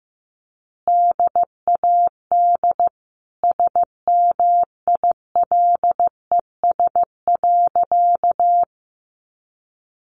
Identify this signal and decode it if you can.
Morse code, heard as a keyed tone, about 14 wpm, 700 Hz. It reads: DAD SMILES.
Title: Easy Phrases 15wpm